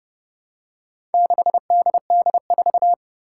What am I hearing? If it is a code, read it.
Morse code, heard as a keyed tone, about 30 wpm, 690 Hz. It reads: 6DD4
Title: Four Challenging Letter-Number Combinations 30wpm